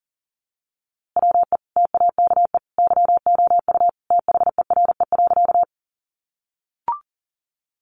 Morse code, heard as a keyed tone, about 40 wpm, 700 Hz. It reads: WE TAKE YOU THERE.
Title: A Sound Of Thunder 40wpm